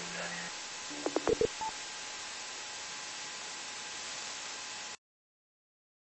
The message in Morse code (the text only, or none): none